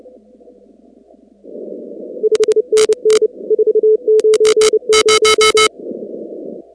Morse code, heard as a keyed tone, about 30 wpm, 430 Hz. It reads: 5NN 490
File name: contests / 2019-cw-wpx-kt5j-k5pi- -2330 0-19